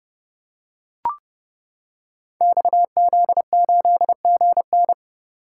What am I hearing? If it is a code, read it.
Morse code, heard as a keyed tone, about 30 wpm, 700 Hz. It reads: XZ8GN
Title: Code Groups - 5 Letter:Numbers - 30wpm